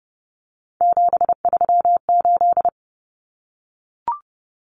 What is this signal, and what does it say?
Morse code, heard as a keyed tone, about 30 wpm, 700 Hz. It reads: 738
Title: Numbers - 3-Digit 30wpm